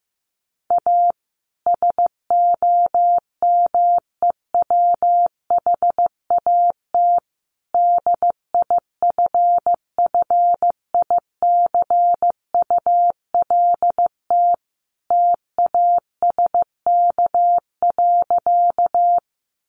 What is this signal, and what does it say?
Morse code, heard as a keyed tone, about 15 wpm, 700 Hz. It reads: A SOMEWHAT DIFFICULT TASK.
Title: Sentences from Top 1300 Words 15wpm